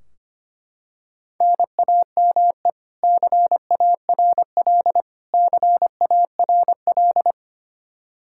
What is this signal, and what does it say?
Morse code, heard as a keyed tone, about 25 wpm, 700 Hz. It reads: NAME CARL CARL
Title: K1USN SST QSO Practice 25wpm